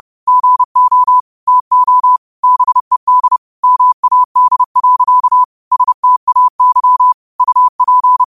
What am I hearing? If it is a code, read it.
Morse code, heard as a keyed tone, about 30 wpm, 1000 Hz. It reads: GO TO BED MAD. STAY UP
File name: qotdfiles / 30wpm